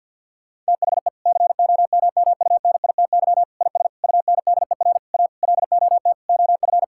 Morse code, heard as a keyed tone, about 50 wpm, 700 Hz. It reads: THE COMMUNITY IS UNDER A LOT OF
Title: Sentences from Top 1000 Words - Review Part 6 - 50wpm